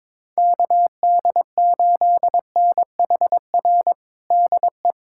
Morse code, heard as a keyed tone, about 22 wpm, 700 Hz. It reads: KD8NHR DE
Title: USA Callsign DE Callsign 22wpm